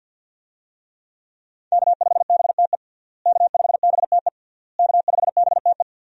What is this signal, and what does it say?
Morse code, heard as a keyed tone, about 52 wpm, 700 Hz. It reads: K5BTE K5BTE K5BTE
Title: USA Call Signs - 5 Characters - Repeated 3x - 50wpm